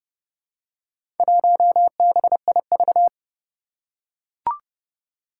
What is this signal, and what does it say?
Morse code, heard as a keyed tone, about 32 wpm, 700 Hz. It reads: 1BIV
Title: Four Challenging Letter-Number Combinations 30wpm